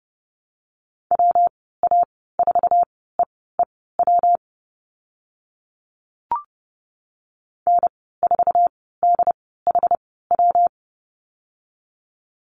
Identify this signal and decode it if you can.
Morse code, heard as a keyed tone, about 30 wpm, spaced as for 17 wpm, 700 Hz. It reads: WA4EEW N4DHW
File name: Lesson 19g - Letter w - Call Signs 30wpm